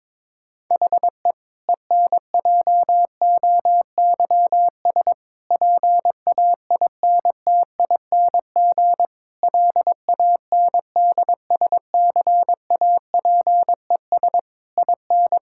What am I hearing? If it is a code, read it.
Morse code, heard as a keyed tone, about 22 wpm, 700 Hz. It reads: HE ENJOYS PAINTING LANDSCAPES IN
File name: Difficult Sentences with Hints - 22wpm